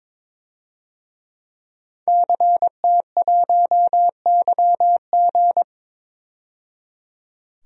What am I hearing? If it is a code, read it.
Morse code, heard as a keyed tone, about 22 wpm, 700 Hz. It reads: CT1YG
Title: DXCC-100 Call Signs - 5 Characters - 22wpm